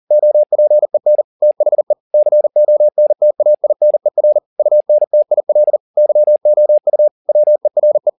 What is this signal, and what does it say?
Morse code, heard as a keyed tone, about 40 wpm, 600 Hz. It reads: OPEN THE CONTAINER UNTIL YOU WERE